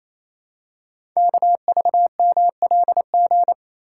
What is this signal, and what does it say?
Morse code, heard as a keyed tone, about 28 wpm, 700 Hz. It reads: KVMLG
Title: Code Groups - 5 Letters - 28wpm